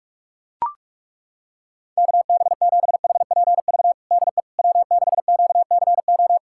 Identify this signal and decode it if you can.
Morse code, heard as a keyed tone, about 42 wpm, 700 Hz. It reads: KB7HPV DE W6QCO
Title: USA Callsign DE Callsign 45wpm